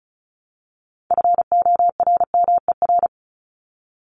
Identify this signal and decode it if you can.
Morse code, heard as a keyed tone, about 34 wpm, 700 Hz. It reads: FORMER
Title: Top 1751-2000 Words 35wpm